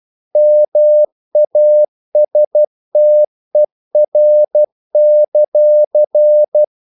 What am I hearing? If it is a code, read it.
Morse code, heard as a keyed tone, about 12 wpm, 600 Hz. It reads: MASTER;
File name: CW / Practice / If12